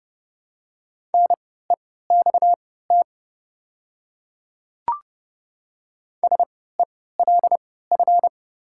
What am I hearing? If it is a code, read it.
Morse code, heard as a keyed tone, about 32 wpm, spaced as for 17 wpm, 700 Hz. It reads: NEXT SELF
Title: 3x Character Spacing - Top 300 Words - 30wpm